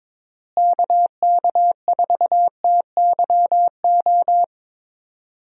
Morse code, heard as a keyed tone, about 22 wpm, 700 Hz. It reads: KK4TYO
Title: USA Call Signs - 6 Characters - Repeated 3x - 22wpm